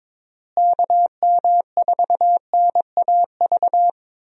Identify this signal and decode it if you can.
Morse code, heard as a keyed tone, about 21 wpm, 700 Hz. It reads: KM4NAV
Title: USA Call Signs - 6 Characters - Repeated 3x - 22wpm